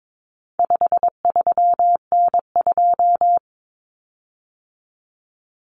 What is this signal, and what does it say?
Morse code, heard as a keyed tone, about 22 wpm, 700 Hz. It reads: 53N2
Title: Common Cut Numbers - 4-Digit 22wpm